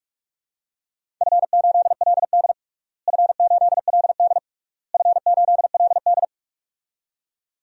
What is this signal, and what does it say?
Morse code, heard as a keyed tone, about 41 wpm, 700 Hz. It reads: F8LD F8LD F8LD
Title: DXCC-100 Call Signs - 4 Characters - Repeated 3x - 45wpm